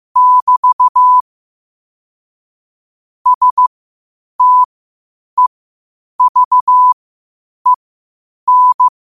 Morse code, heard as a keyed tone, about 15 wpm, spaced as for 8 wpm, 1000 Hz. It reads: <BT> STEVEN
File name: qotdfiles / 8wpm